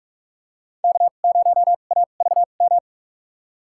Morse code, heard as a keyed tone, about 46 wpm, spaced as for 33 wpm, 700 Hz. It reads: K0AVM
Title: USA Call Signs - 5 Characters - 2x Character Spacing - 45wpm